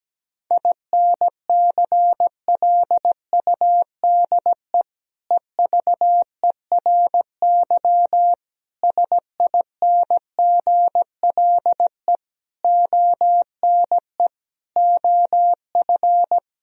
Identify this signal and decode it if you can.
Morse code, heard as a keyed tone, about 17 wpm, 700 Hz. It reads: INCLUDE EVERY SINGLE ONE OF
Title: Sentences from Top 1000 Words - Review Part 4 - 17wpm